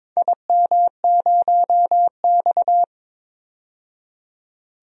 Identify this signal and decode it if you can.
Morse code, heard as a keyed tone, about 22 wpm, 700 Hz. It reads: IM0X
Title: DXCC-100 Call Signs - 4 Characters - 22wpm